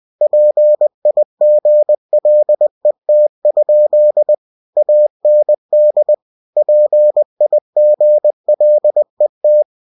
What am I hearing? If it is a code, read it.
Morse code, heard as a keyed tone, about 20 wpm, 600 Hz. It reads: PIGLET? AND PIGLET
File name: cw-books / winnie-the-pooh / winnie-20wpm-page-0091